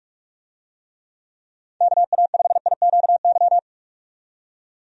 Morse code, heard as a keyed tone, about 46 wpm, 700 Hz. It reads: KA5IQY
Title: USA Call Signs - 6 Characters - 45wpm